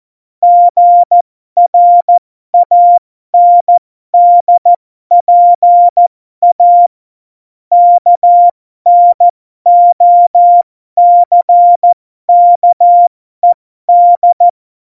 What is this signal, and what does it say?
Morse code, heard as a keyed tone, about 14 wpm, spaced as for 12 wpm, 700 Hz. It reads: GRANDPA KNOCKED
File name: Morsekurs / Witze